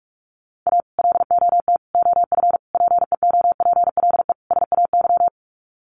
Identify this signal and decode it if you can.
Morse code, heard as a keyed tone, about 44 wpm, 700 Hz. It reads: A LOT OF PEOPLE SAY
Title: Sets of 5 Words Encore2 45wpm